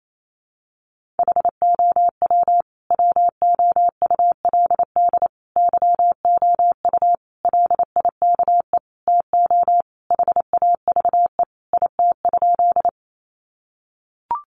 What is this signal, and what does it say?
Morse code, heard as a keyed tone, about 28 wpm, 700 Hz. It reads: HOW WOULD YOU LIKE TO HAVE IT?
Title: Sentences from Top 100 Words 28wpm